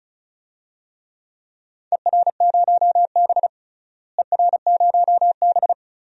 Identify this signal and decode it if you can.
Morse code, heard as a keyed tone, about 34 wpm, 700 Hz. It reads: ER0B ER0B
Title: DXCC-100 Call Signs - 4 Characters - Repeated 2x - 35wpm